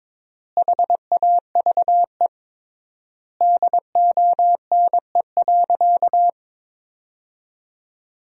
Morse code, heard as a keyed tone, about 22 wpm, 700 Hz. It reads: HAVE DONE.
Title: 3x Word-Spacing - Sentences from Top 500 Words - 22wpm